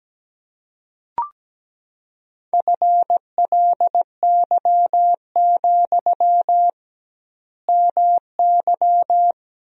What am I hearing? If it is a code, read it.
Morse code, heard as a keyed tone, about 17 wpm, 700 Hz. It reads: FLY, MY
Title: 2x Word-Spacing - Sentences from Top 500 Words - 17wpm